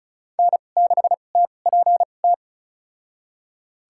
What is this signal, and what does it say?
Morse code, heard as a keyed tone, about 36 wpm, spaced as for 25 wpm, 700 Hz. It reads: N6TPT
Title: USA Call Signs - 5 Characters - 2x Character Spacing - 35wpm